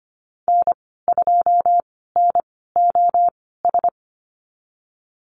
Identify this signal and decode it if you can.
Morse code, heard as a keyed tone, about 25 wpm, spaced as for 16 wpm, 700 Hz. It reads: N2NOS
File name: Lesson 15g - Number 5 - Call Signs 25wpm